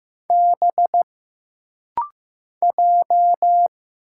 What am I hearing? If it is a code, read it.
Morse code, heard as a keyed tone, about 15 wpm, 700 Hz. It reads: B J
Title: Single Letter-Number Alt - Rapid-Fire 15wpm